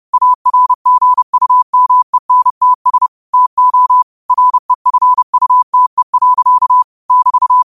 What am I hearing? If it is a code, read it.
Morse code, heard as a keyed tone, about 30 wpm, 1000 Hz. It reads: ARGUMENTS TO REFUTE. <BT>
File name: qotdfiles / 30wpm